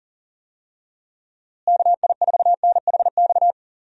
Morse code, heard as a keyed tone, about 40 wpm, 700 Hz. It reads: KI4NHX
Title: USA Call Signs - 6 Characters - Repeated 2x - 40wpm